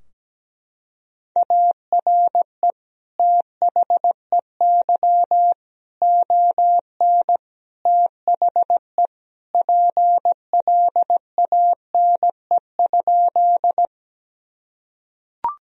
Morse code, heard as a keyed tone, about 17 wpm, 700 Hz. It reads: ARE THEY ON THE PLANE?
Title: Sentences from Top 1000 Words - Review Part 3 - 17wpm